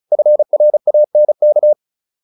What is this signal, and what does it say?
Morse code, heard as a keyed tone, about 34 wpm, 600 Hz. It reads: FRANK
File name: QSO208 35